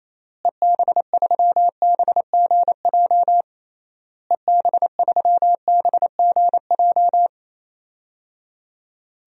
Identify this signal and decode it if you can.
Morse code, heard as a keyed tone, about 28 wpm, 700 Hz. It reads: EB3BGJ EB3BGJ
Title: DXCC-100 Call Signs - 6 Characters - Repeated 3x - 28wpm